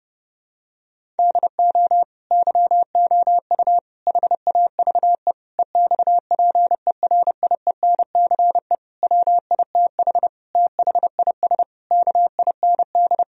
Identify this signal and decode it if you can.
Morse code, heard as a keyed tone, about 30 wpm, 700 Hz. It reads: DO YOU HAVE EXPERIENCE WITH THIS KIND
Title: Sentences from Top 1000 Words - Review Part 3 - 30wpm